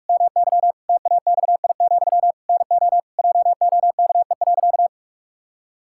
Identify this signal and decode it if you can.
Morse code, heard as a keyed tone, about 45 wpm, 700 Hz. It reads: MY TAXI, NO JOKE.
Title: Pangram Challenge v2 - 45wpm